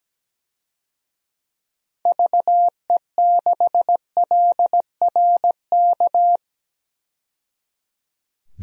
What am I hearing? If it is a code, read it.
Morse code, heard as a keyed tone, about 17 wpm, 700 Hz. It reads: VE6LRK
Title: DXCC-100 Call Signs - 6 Characters - 17wpm